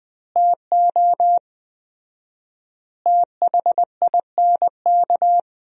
Morse code, heard as a keyed tone, about 20 wpm, 700 Hz. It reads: TO THINK
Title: 4x Word-Spacing - Sentences from Top 400 Words - 20wpm